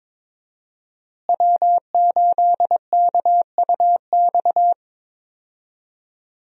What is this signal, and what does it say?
Morse code, heard as a keyed tone, about 22 wpm, 700 Hz. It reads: W8KUX
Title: USA Call Signs - 5 Characters - Repeated 2x - 22wpm